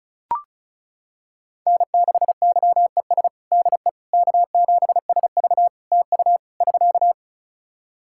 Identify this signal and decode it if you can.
Morse code, heard as a keyed tone, about 36 wpm, 700 Hz. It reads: N6YES DE K7SV TU <SK>
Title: K1USN SST QSO Practice 35wpm